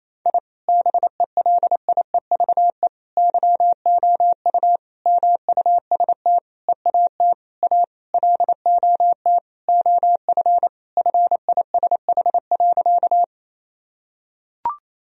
Morse code, heard as a keyed tone, about 27 wpm, 700 Hz. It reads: I BELIEVE YOU MUST EAT A LOT OF FISH.
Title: Sentences from Top 1000 Words - Review Part 3 - 28wpm